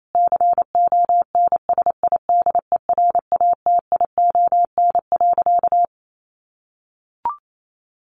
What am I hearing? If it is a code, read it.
Morse code, heard as a keyed tone, about 28 wpm, 700 Hz. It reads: CONSIDERATION.
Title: Sentences from Top 1500 Words 28wpm